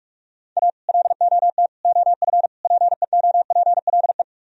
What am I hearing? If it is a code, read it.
Morse code, heard as a keyed tone, about 44 wpm, 700 Hz. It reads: A LOT OF PEOPLE